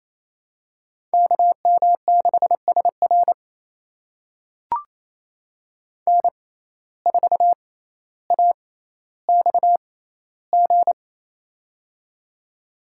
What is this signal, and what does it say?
Morse code, heard as a keyed tone, about 28 wpm, spaced as for 14 wpm, 700 Hz. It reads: KM6SR N4AXG
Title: USA Call Signs - 5 Characters - 6x Character Spacing - 28wpm